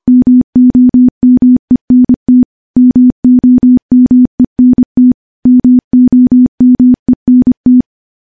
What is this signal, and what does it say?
Morse code, heard as a keyed tone, about 25 wpm, 250 Hz. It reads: MOMENT MOMENT MOMENT
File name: morse / morse-training-25WPM